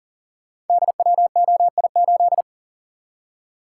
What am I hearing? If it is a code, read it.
Morse code, heard as a keyed tone, about 42 wpm, 700 Hz. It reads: DWOI8
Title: Code Groups - 5 Letter:Numbers - 40wpm